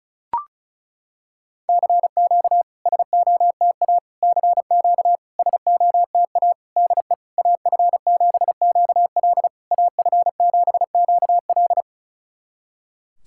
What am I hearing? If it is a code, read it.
Morse code, heard as a keyed tone, about 36 wpm, 700 Hz. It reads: CQ SOTA CQ SOTA DE AF7QL AF7QL
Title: SOTA QSO Practice 35wpm